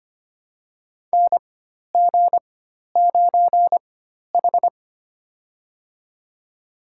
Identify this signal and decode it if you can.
Morse code, heard as a keyed tone, about 25 wpm, spaced as for 12 wpm, 700 Hz. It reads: NG9H
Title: USA Call Signs - 4 Characters - 4x Character Spacing - 25wpm